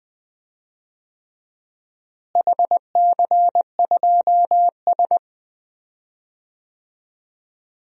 Morse code, heard as a keyed tone, about 20 wpm, 700 Hz. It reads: HC2S
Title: DXCC-100 Call Signs - 4 Characters - Repeated 3x - 20wpm